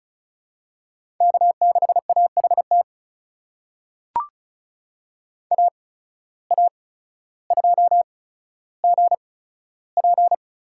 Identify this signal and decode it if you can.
Morse code, heard as a keyed tone, about 34 wpm, spaced as for 15 wpm, 700 Hz. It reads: K6AHT AA2GP
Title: USA Call Signs - 5 Characters - 8x Character Spacing - 35wpm